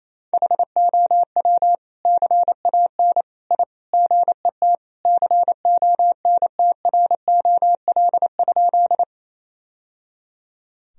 Morse code, heard as a keyed tone, about 28 wpm, 700 Hz. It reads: HOW CAN I GET CONTROL?
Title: Sentences from Top 1000 Words - Review Part 3 - 28wpm